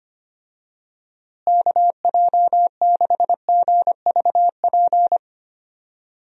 Morse code, heard as a keyed tone, about 25 wpm, 700 Hz. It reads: KJ6GVP
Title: USA Call Signs - 6 Characters - Repeated 3x - 25wpm